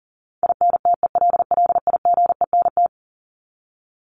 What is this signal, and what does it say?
Morse code, heard as a keyed tone, about 40 wpm, 700 Hz. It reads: INTELLIGENT
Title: Top 751-1000 Words 40wpm